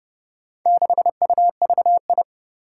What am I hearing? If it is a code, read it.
Morse code, heard as a keyed tone, about 30 wpm, 700 Hz. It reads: 6UVI
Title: Four Challenging Letter-Number Combinations 30wpm